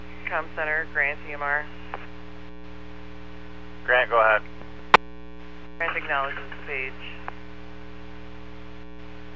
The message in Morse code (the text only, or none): none